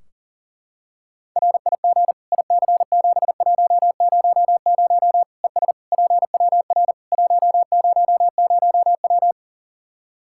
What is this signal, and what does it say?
Morse code, heard as a keyed tone, about 40 wpm, 700 Hz. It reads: RIG IC7100 ES PWR 100W
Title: Practice QSO - Exchange 2 - Info - 40wpm